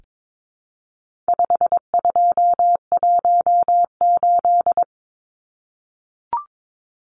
Morse code, heard as a keyed tone, about 22 wpm, 700 Hz. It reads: 5218
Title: Numbers - 4-Digit 22wpm